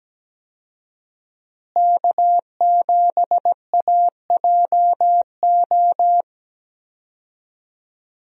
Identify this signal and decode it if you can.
Morse code, heard as a keyed tone, about 17 wpm, 700 Hz. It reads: K7AJO
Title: USA Call Signs - 5 Characters - Repeated 3x - 17wpm